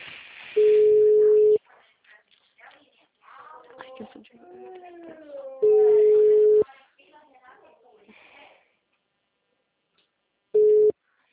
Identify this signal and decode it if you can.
Ringback tone (Europe, 425 Hz), with speech.